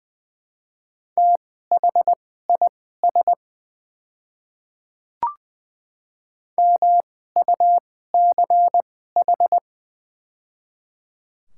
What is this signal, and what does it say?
Morse code, heard as a keyed tone, about 20 wpm, spaced as for 15 wpm, 700 Hz. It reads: THIS MUCH
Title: 2x Character Spacing - Top 200 Words - 20wpm